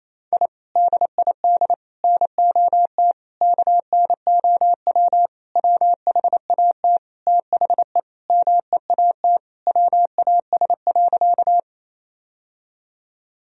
Morse code, heard as a keyed tone, about 28 wpm, 700 Hz. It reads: I DID NOT KNOW WHAT THE MEAT WAS.
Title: Sentences from Top 1000 Words - Review Part 3 - 28wpm